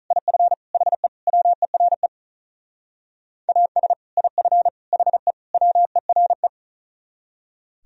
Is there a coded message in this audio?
Yes: Morse code from 0.1 to 6.5 s, heard as a keyed tone, about 44 wpm, 700 Hz. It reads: IF HE WERE AS IF HE WERE